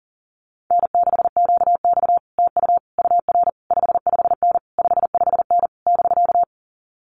Morse code, heard as a keyed tone, about 40 wpm, 700 Hz. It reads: N6QX TU UR 55N 55N <BK>
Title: Common QSO Phrases Encore 40wpm